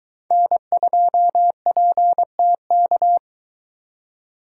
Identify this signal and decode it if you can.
Morse code, heard as a keyed tone, about 22 wpm, 700 Hz. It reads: N2PTK